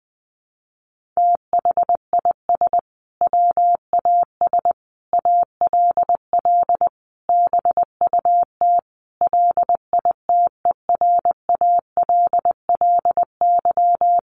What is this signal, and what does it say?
Morse code, heard as a keyed tone, about 20 wpm, 700 Hz. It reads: THIS WAS ALL BUT LITERALLY
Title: Sentences from Top 1400 Words 20wpm